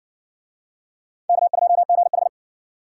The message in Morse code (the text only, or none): K2GH